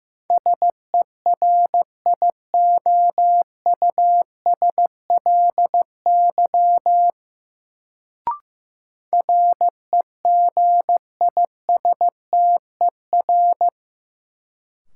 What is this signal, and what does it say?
Morse code, heard as a keyed tone, about 15 wpm, 700 Hz. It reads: SERIOUSLY REGISTER
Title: Sentences from Top 1200 Words - New Word Drill - 15wpm